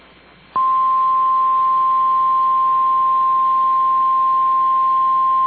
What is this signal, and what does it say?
Emergency Alert System (SAME) tones (the 1050 Hz attention tone). It is heard from 0.8 to 4.8 s.
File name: EAS Logs / KMNR